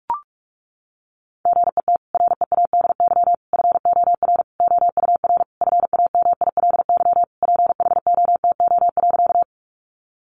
Morse code, heard as a keyed tone, about 44 wpm, 700 Hz. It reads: GET READY FOR OUR FAMILY PHOTO.
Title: Sentences from Top 1200 Words 45wpm